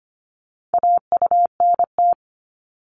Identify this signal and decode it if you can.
Morse code, heard as a keyed tone, about 25 wpm, 700 Hz. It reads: AUNT
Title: Multi-letter Phonemes 25wpm